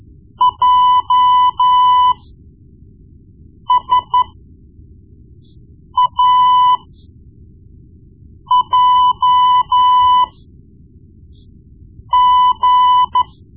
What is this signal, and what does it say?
Morse code, heard as a keyed tone, about 12 wpm, spaced as for 4 wpm, 1000 Hz. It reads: JSAJG
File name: Morse Lessons / Morse Lessons / Tape-1B